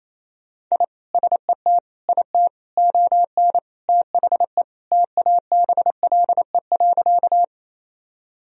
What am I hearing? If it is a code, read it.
Morse code, heard as a keyed tone, about 28 wpm, 700 Hz. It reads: I SET IT ON THE TABLE.
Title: Sentences from Top 1000 Words - Review Part 2 - 28wpm